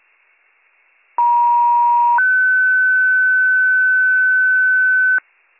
Two-tone page, heard as a one-second tone, then a three-second tone, about 950 then 1500 Hz, with noise. It is heard from 1.2 to 5.2 s.